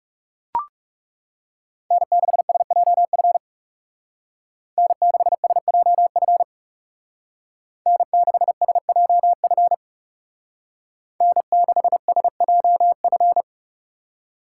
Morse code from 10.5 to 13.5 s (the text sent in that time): N6SJF